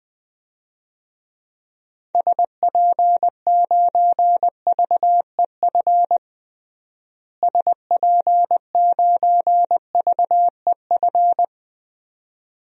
Morse code, heard as a keyed tone, about 21 wpm, 700 Hz. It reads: SP9VEF SP9VEF
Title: DXCC-100 Call Signs - 6 Characters - Repeated 2x - 20wpm